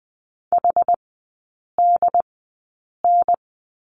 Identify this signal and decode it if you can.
Morse code, heard as a keyed tone, about 20 wpm, spaced as for 8 wpm, 700 Hz. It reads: HDN
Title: 074 Lesson 13e - Letter L - Three Characters 20wpm